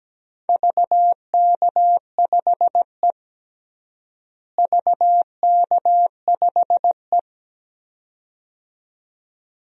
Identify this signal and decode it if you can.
Morse code, heard as a keyed tone, about 17 wpm, 700 Hz. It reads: VK5E VK5E
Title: DXCC-100 Call Signs - 4 Characters - Repeated 2x - 17wpm